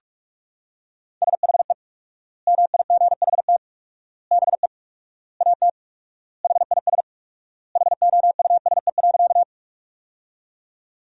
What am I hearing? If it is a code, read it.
Morse code, heard as a keyed tone, about 46 wpm, 700 Hz. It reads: SHE MIGHT BE AT HIS HOUSE.
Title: 4x Word-Spacing - Sentences from Top 300 Words - 45wpm